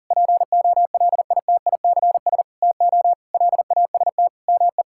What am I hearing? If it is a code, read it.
Morse code, heard as a keyed tone, about 40 wpm, 700 Hz. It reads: POLITICS TO LAST ME
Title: Sentences from Top 1000 Words - Review Part 6 - 40wpm